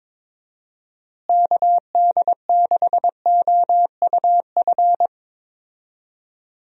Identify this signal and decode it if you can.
Morse code, heard as a keyed tone, about 22 wpm, 700 Hz. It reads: KD6OUF
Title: USA Call Signs - 6 Characters - Repeated 3x - 22wpm